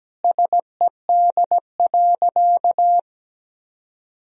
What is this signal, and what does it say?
Morse code, heard as a keyed tone, about 17 wpm, 700 Hz. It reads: SED.
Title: Sentences from Top 400 Words and Abbreviations 17wpm